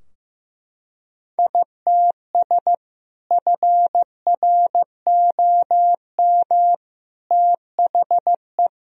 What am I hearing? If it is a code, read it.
Morse code, heard as a keyed tone, about 15 wpm, 700 Hz. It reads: ITS FROM THE
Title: Sentences from Top 1000 Words - Review Part 2 - 15wpm